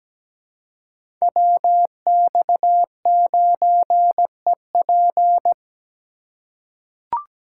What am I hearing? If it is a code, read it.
Morse code, heard as a keyed tone, about 17 wpm, 700 Hz. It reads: WX9EP
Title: USA Call Signs - 5 Characters - 3x Character Spacing - 17wpm